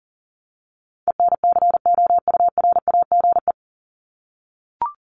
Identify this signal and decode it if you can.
Morse code, heard as a keyed tone, about 40 wpm, 700 Hz. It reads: ENCOURAGE
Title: Top 1251-1500 Words 40wpm